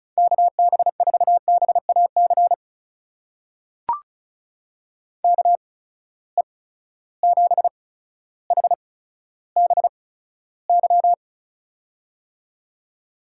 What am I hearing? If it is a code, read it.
Morse code, heard as a keyed tone, about 36 wpm, spaced as for 15 wpm, 700 Hz. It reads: KB4BAC KE7HBY
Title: USA Call Signs - 6 Characters - 8x Character Spacing - 35wpm